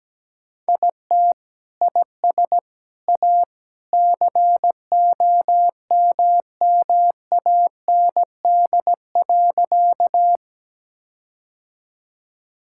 Morse code, heard as a keyed tone, about 17 wpm, 700 Hz. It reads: IT IS A COMMAND.
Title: Sentences from Top 1600 Words 17wpm